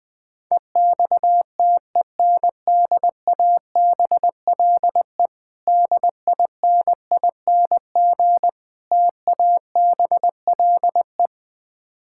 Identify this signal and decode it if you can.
Morse code, heard as a keyed tone, about 20 wpm, 700 Hz. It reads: EXTENDABLE DINING TABLE
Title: Difficult Words - Speed-Racing with Hints - 20wpm